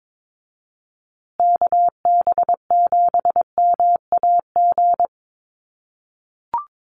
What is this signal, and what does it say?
Morse code, heard as a keyed tone, about 22 wpm, 700 Hz. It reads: KB7MAG
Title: USA Call Signs Repeated 3x 22wpm